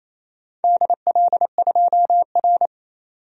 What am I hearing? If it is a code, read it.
Morse code, heard as a keyed tone, about 28 wpm, 700 Hz. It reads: DL2R
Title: DXCC-100 Call Signs - 4 Characters - 28wpm